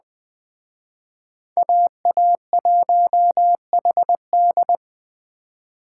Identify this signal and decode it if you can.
Morse code, heard as a keyed tone, about 20 wpm, 700 Hz. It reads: AA1HD
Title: Speed Racing - USA 2x2 Call Signs 17wpm